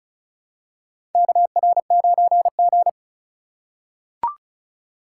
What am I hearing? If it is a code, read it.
Morse code, heard as a keyed tone, about 36 wpm, 700 Hz. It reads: KR9G
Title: USA Call Signs - 4x Character Spacing - 35wpm